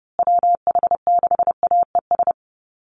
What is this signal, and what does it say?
Morse code, heard as a keyed tone, about 30 wpm, 700 Hz. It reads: WH6AES
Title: USA Call Signs Repeated 3x 30wpm